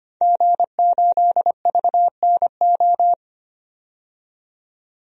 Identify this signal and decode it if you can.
Morse code, heard as a keyed tone, about 25 wpm, 700 Hz. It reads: G8VNO
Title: DXCC-100 Call Signs - 5 Characters - 25wpm